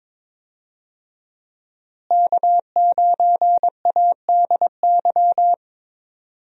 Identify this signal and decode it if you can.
Morse code, heard as a keyed tone, about 22 wpm, 700 Hz. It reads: K9ADY
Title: USA Call Signs - 5 Characters - Repeated 3x - 22wpm